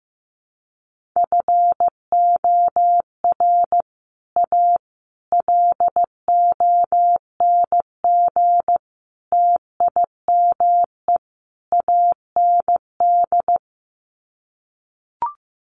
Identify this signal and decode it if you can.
Morse code, heard as a keyed tone, about 15 wpm, 700 Hz. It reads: FOR A LONG TIME AND
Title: Sets of 5 Words Encore2 15wpm